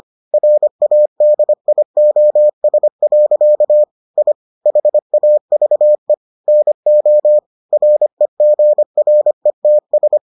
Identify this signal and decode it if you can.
Morse code, heard as a keyed tone, about 25 wpm, 600 Hz. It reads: RADIOS. I HAVE NO REGRETS